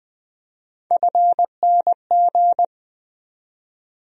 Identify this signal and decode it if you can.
Morse code, heard as a keyed tone, about 20 wpm, 700 Hz. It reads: FNG
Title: USA Call Sign Postfixes - 3 Letters 20wpm